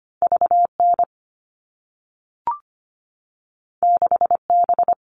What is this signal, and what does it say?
Morse code, heard as a keyed tone, about 25 wpm, 700 Hz. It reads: VN 6B
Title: All Cut Numbers - 2-Digit 25wpm